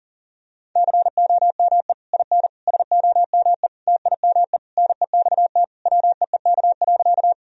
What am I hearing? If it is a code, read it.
Morse code, heard as a keyed tone, about 40 wpm, 700 Hz. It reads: COME IN SOME TIME NEXT WEEK.
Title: Sentences from Top 1000 Words - Review Part 2 - 40wpm